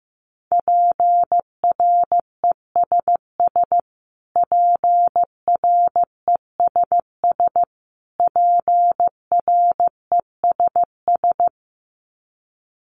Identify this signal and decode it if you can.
Morse code, heard as a keyed tone, about 15 wpm, 700 Hz. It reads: PRESS PRESS PRESS
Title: Top 300 Words Repeated 15wpm